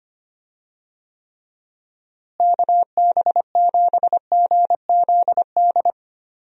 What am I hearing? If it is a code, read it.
Morse code, heard as a keyed tone, about 25 wpm, 700 Hz. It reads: KB7GZD
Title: USA Call Signs - 6 Characters - Repeated 2x - 25wpm